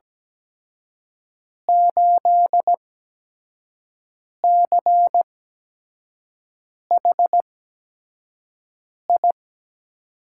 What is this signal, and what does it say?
Morse code, heard as a keyed tone, about 17 wpm, spaced as for 5 wpm, 700 Hz. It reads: 8CHI
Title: USA Call Signs - 6 Characters - 8x Character Spacing - 17wpm